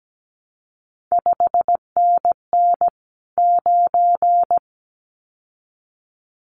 Morse code, heard as a keyed tone, about 17 wpm, 700 Hz. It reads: 5NN 9
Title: CQ WW DX Contest Exchange - 17wpm